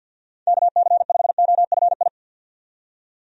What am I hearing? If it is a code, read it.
Morse code, heard as a keyed tone, about 46 wpm, 700 Hz. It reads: KC5OFI
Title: USA Call Signs - 6 Characters - 50wpm